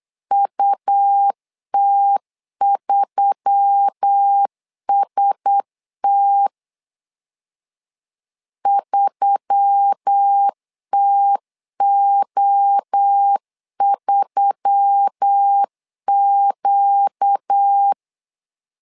Morse code, heard as a keyed tone, about 8 wpm, 800 Hz. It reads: UT3ST 3TO3Q
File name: LEZIONI / 4 Q1S3 8wpm 3037